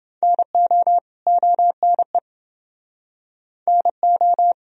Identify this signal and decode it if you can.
Morse code, heard as a keyed tone, about 29 wpm, 700 Hz. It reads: NO ONE NO